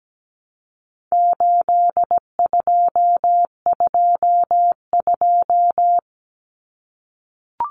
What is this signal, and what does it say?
Morse code, heard as a keyed tone, about 17 wpm, 700 Hz. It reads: 8222
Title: Common Cut Numbers - 4-Digit 17wpm